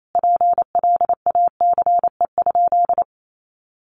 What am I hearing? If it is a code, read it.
Morse code, heard as a keyed tone, about 28 wpm, 700 Hz. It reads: PLACE?
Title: Sentences from Top 1500 Words 28wpm